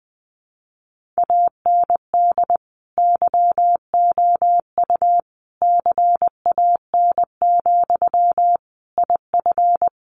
Morse code, heard as a keyed tone, about 20 wpm, 700 Hz. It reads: AND YOU CAN, IF
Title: Sentences from Top 1500 Words 20wpm